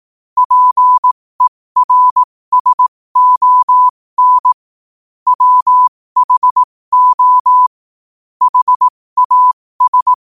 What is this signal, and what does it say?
Morse code, heard as a keyed tone, about 18 wpm, spaced as for 15 wpm, 1000 Hz. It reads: PERSON WHO HAS
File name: qotdfiles / 15wpm